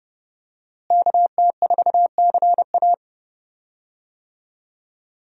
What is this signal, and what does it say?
Morse code, heard as a keyed tone, about 31 wpm, 700 Hz. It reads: KT4CA
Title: USA Call Signs - 5 Characters - Repeated 2x - 30wpm